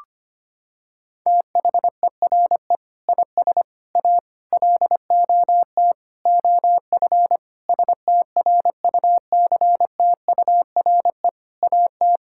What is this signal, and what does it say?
Morse code, heard as a keyed tone, about 25 wpm, 700 Hz. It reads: THERE IS A LOT OF STRUCTURE AT
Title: Sentences from Top 1000 Words - Review Part 6 - 25wpm